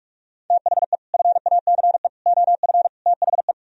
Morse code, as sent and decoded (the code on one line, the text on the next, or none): - .... / ..-. .- -.-. / --- ..-. / - .... .
THE FACE OF THE